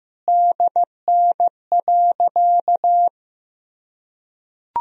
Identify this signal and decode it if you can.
Morse code, heard as a keyed tone, about 15 wpm, 700 Hz. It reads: DN.
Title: Sentences from Top 500 Words and Abbreviations 15wpm